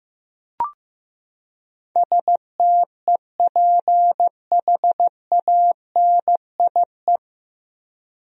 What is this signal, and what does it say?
Morse code, heard as a keyed tone, about 16 wpm, 700 Hz. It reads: STEPHANIE
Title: USA Common Names - 15wpm